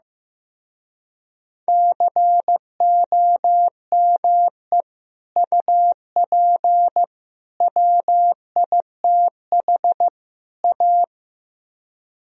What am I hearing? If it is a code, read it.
Morse code, heard as a keyed tone, about 16 wpm, 700 Hz. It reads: COME UP WITH A
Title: Speed-Racing - Sets of 4 Words 15wpm